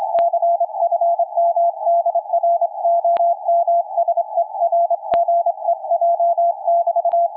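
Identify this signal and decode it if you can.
Morse code, heard as a keyed tone, about 26 wpm, 680 Hz, with radio noise. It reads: CFM DR OM SERGEJ <BT>
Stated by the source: QRP station's CW heard around 7029.2 kHz; signal fading from a solid 559 down to complete dropout.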